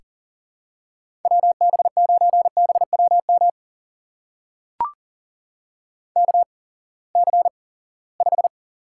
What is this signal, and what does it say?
Morse code, heard as a keyed tone, about 39 wpm, spaced as for 22 wpm, 700 Hz. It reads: WB9BWM KC5
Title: USA Call Signs - 6 Characters - 8x Character Spacing - 40wpm